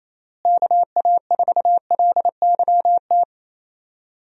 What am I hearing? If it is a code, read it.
Morse code, heard as a keyed tone, about 27 wpm, 700 Hz. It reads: KA4LYT
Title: USA Call Signs - 6 Characters - Repeated 3x - 28wpm